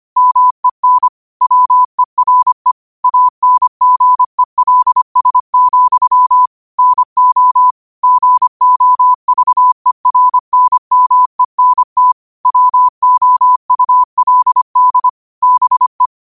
Morse code, heard as a keyed tone, about 25 wpm, 1000 Hz. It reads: MEN WERE ANGELS, NO GOVERNMENT WOULD BE
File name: qotdfiles / 25wpm